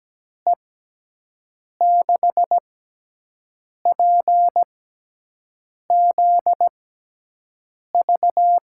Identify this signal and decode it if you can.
Morse code, heard as a keyed tone, about 17 wpm, spaced as for 6 wpm, 700 Hz. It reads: E6PZV
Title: USA Call Signs - 6 Characters - 6x Character Spacing - 17wpm